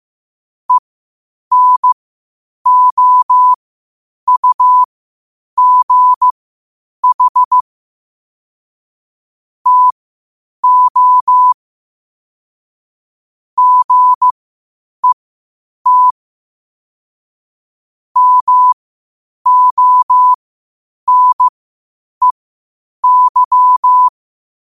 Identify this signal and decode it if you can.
Morse code, heard as a keyed tone, about 15 wpm, spaced as for 8 wpm, 1000 Hz. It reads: ENOUGH TO GET MONEY
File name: qotdfiles / 8wpm